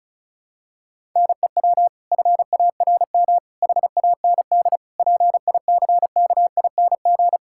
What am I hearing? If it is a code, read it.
Morse code, heard as a keyed tone, about 36 wpm, 700 Hz. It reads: NEW FARM HAND PICKING
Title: Pangram Challenge v2 - 35wpm